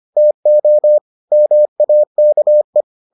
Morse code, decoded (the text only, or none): TO MAKE